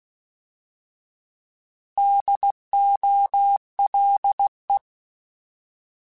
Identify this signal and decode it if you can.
Morse code, heard as a keyed tone, about 16 wpm, 800 Hz. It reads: DOLE
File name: Alphabetical / Alphabetical mp3s / Ds @15